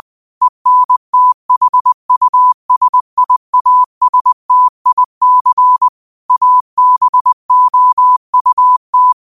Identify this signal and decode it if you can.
Morse code, heard as a keyed tone, about 20 wpm, 1000 Hz. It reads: ENTHUSIASTIC ABOUT